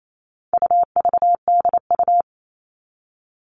Morse code, heard as a keyed tone, about 28 wpm, 700 Hz. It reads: UVDU
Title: All Cut Numbers - 4-Digit 28wpm